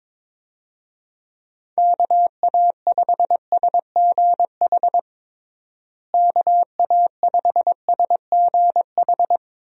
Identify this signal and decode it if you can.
Morse code, heard as a keyed tone, about 22 wpm, 700 Hz. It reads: KA5SGH KA5SGH
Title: USA Call Signs - 6 Characters - Repeated 2x - 22wpm